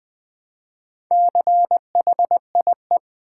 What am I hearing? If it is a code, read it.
Morse code, heard as a keyed tone, about 20 wpm, 700 Hz. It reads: CHIE
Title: Build A Word - 1 Letter At A Time - 5-Letter Words - 20wpm